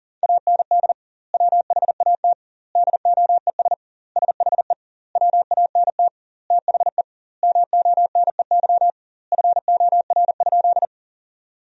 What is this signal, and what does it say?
Morse code, heard as a keyed tone, about 40 wpm, 700 Hz. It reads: AND WHAT DOES SHE WANT THE MONEY FOR?
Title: 2x Word-Spacing - Sentences from Top 400 Words - 40wpm